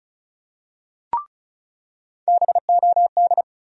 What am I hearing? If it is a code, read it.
Morse code, heard as a keyed tone, about 34 wpm, 700 Hz. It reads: BOD
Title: USA Call Sign Postfixes - 3 Letters 35wpm